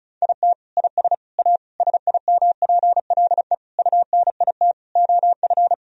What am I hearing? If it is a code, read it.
Morse code, heard as a keyed tone, about 34 wpm, 700 Hz. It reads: IT IS A SIMPLE UNIT OF
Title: Sentences from Top 1000 Words - Review Part 2 - 35wpm